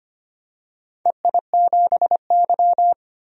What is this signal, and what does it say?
Morse code, heard as a keyed tone, about 25 wpm, 700 Hz. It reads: EI7Y
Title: DXCC-100 Call Signs - 4 Characters - 25wpm